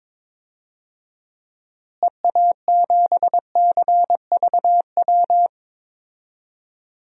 Morse code, heard as a keyed tone, about 22 wpm, 700 Hz. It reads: EA7CVW
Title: DXCC-100 Call Signs - 6 Characters - Repeated 3x - 22wpm